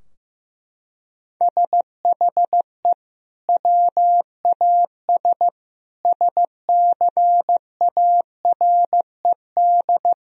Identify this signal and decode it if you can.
Morse code, heard as a keyed tone, about 15 wpm, 700 Hz. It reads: SHE WAS SCARED
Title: Sentences from Top 1000 Words - Review Part 4 - 15wpm